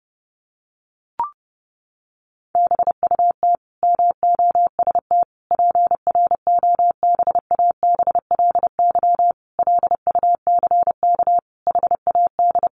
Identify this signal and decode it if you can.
Morse code, heard as a keyed tone, about 30 wpm, 700 Hz. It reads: BUT MOST PROBABLY LUCK HAD
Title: Sentences from Top 1500 Words 30wpm